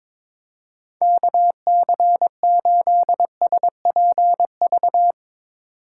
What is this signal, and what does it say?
Morse code, heard as a keyed tone, about 22 wpm, 700 Hz. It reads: KC8SPV
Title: USA Call Signs - 6 Characters - 2x Character Spacing - 22wpm